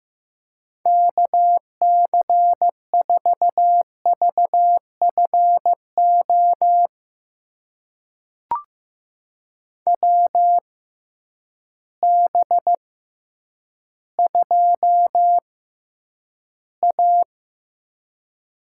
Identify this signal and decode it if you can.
Morse code, heard as a keyed tone, about 15 wpm, spaced as for 9 wpm, 700 Hz. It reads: KC4VFO WB2A
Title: USA Call Signs - 6 Characters - 6x Character Spacing - 15wpm